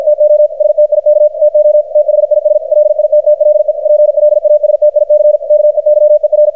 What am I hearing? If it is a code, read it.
Morse code, heard as a keyed tone, about 53 wpm, 600 Hz, with radio noise. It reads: TO STIK TO THIS LITTLE PLASTIC KEYER
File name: archived / CRSnet